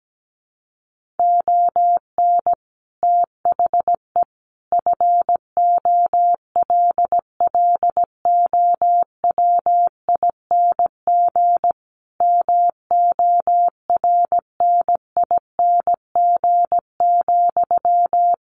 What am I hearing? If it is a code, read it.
Morse code, heard as a keyed tone, about 17 wpm, 700 Hz. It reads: ON THE FOLLOWING MORNING,
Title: Sentences from Top 1400 Words 17wpm